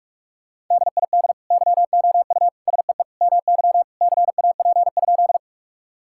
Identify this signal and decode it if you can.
Morse code, heard as a keyed tone, about 46 wpm, 700 Hz. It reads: DID YOU SEE MY CAP?
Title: Sentences from Top 1600 Words 45wpm